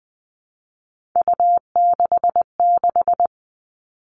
Morse code, heard as a keyed tone, about 20 wpm, 700 Hz. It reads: U66
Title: Three Challenging Letter-Number Combinations 20wpm